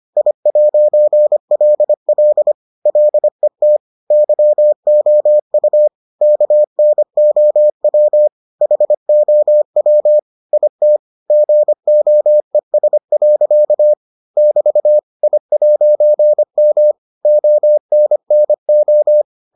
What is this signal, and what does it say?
Morse code, heard as a keyed tone, about 25 wpm, 600 Hz. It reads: I'LL LET YOU KNOW HOW IT GOES. <BT> I'M ONNO